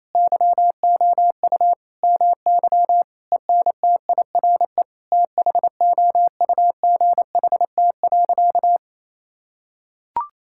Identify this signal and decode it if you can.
Morse code, heard as a keyed tone, about 28 wpm, 700 Hz. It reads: YOU MY ENTIRE THOUGHT.
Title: Sentences from Top 1000 Words - Review Part 5 - 28wpm